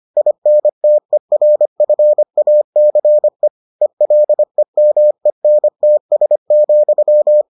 Morse code, heard as a keyed tone, about 25 wpm, 600 Hz. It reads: INTERFACE ELEMENTS,